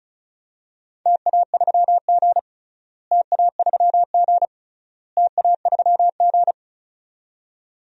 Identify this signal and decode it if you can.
Morse code, heard as a keyed tone, about 36 wpm, 700 Hz. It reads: TA3G TA3G TA3G
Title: DXCC-100 Call Signs - 4 Characters - Repeated 3x - 35wpm